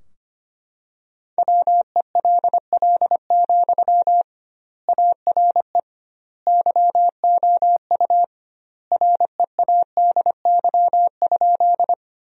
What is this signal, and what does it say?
Morse code, heard as a keyed tone, about 25 wpm, 700 Hz. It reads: WELL, ARE YOU READY?
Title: 2x Word-Spacing - Sentences from Top 400 Words - 25wpm